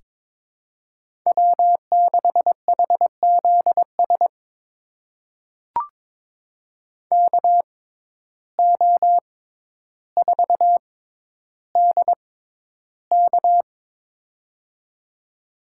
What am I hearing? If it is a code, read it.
Morse code, heard as a keyed tone, about 22 wpm, spaced as for 11 wpm, 700 Hz. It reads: W6HZS KO4DK
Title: USA Call Signs - 5 Characters - 6x Character Spacing - 22wpm